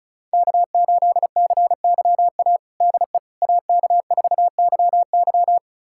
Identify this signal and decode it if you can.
Morse code, heard as a keyed tone, about 34 wpm, 690 Hz. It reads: K8CYA DE AK4YY
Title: USA Callsign DE Callsign - 5 Characters - 35wpm